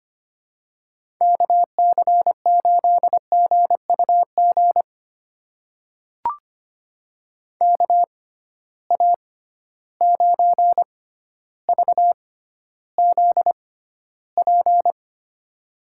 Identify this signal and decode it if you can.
Morse code, heard as a keyed tone, about 25 wpm, spaced as for 13 wpm, 700 Hz. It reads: KC8GUG KA9VZP
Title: USA Call Signs - 6 Characters - 6x Character Spacing - 25wpm